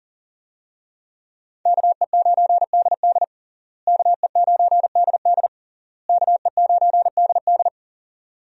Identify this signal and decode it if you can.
Morse code, heard as a keyed tone, about 40 wpm, 700 Hz. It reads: KE9DD KE9DD KE9DD
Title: USA Call Signs - 5 Characters - Repeated 3x - 40wpm